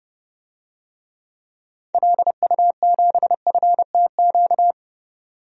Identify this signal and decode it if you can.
Morse code, heard as a keyed tone, about 30 wpm, 700 Hz. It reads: LU7FTQ
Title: DXCC-100 Call Signs - 6 Characters - 30wpm